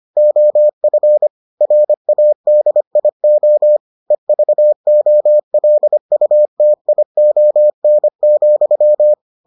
Morse code, heard as a keyed tone, about 25 wpm, 600 Hz. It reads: OF RADIO EVOLUTION,